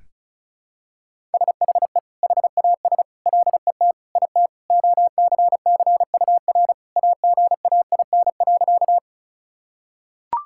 Morse code, heard as a keyed tone, about 34 wpm, 700 Hz. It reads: SHE HAS LET IT OCCUR AGAIN.
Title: Sentences from Top 1000 Words - Review Part 6 - 35wpm